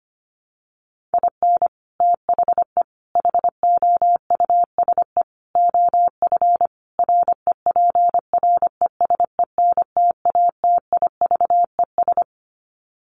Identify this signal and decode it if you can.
Morse code, heard as a keyed tone, about 25 wpm, 700 Hz. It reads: IN THE HOUSE OF REPRESENTATIVES
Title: Sets of 5 Words Encore 25wpm